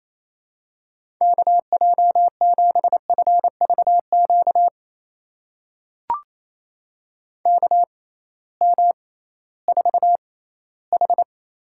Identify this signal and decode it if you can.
Morse code, heard as a keyed tone, about 28 wpm, spaced as for 16 wpm, 700 Hz. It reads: KJ7FVQ KM4H
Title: USA Call Signs - 6 Characters - 6x Character Spacing - 28wpm